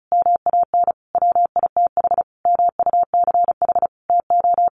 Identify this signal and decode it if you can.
Morse code, heard as a keyed tone, about 34 wpm, 700 Hz. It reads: MAN WITH MUCH TO
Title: Sentences from Top 1400 Words 35wpm